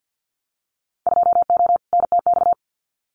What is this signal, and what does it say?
Morse code, heard as a keyed tone, about 50 wpm, 700 Hz. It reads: 2O NTX
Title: Winter Field Day Exchanges 50wpm